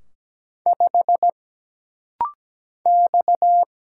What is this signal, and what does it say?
Morse code, heard as a keyed tone, about 17 wpm, 700 Hz. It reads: 5 X
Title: Single Letter-Number Alt - Rapid-Fire 17wpm